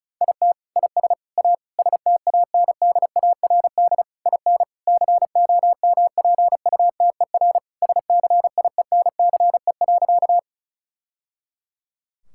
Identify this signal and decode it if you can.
Morse code, heard as a keyed tone, about 36 wpm, 700 Hz. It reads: IT IS A STANDARD IN COMPUTER SCIENCE.
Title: Sentences from Top 1000 Words - Review Part 6 - 35wpm